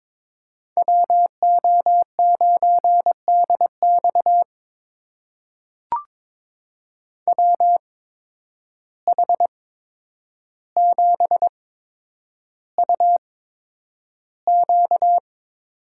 Morse code, heard as a keyed tone, about 22 wpm, spaced as for 9 wpm, 700 Hz. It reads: WO9DX WH7UQ
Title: USA Call Signs - 5 Characters - 8x Character Spacing - 22wpm